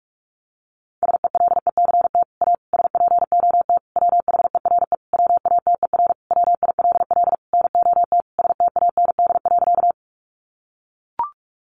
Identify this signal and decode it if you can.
Morse code, heard as a keyed tone, about 44 wpm, 700 Hz. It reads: SELECT A SPOT WHERE WATER WILL NOT STAND.
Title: Sentences from Top 1300 Words 45wpm